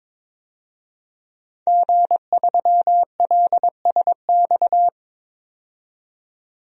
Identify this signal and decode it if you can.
Morse code, heard as a keyed tone, about 22 wpm, 700 Hz. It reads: G3LSX
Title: DXCC-100 Call Signs - 5 Characters - 22wpm